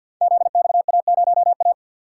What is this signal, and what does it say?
Morse code, heard as a keyed tone, about 51 wpm, 700 Hz. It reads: ZXA0A
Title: Code Groups - 5 Letter:Numbers - 50wpm